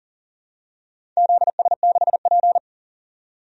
Morse code, heard as a keyed tone, about 37 wpm, 700 Hz. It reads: ZS6P
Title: DXCC-100 Call Signs - 4 Characters - 40wpm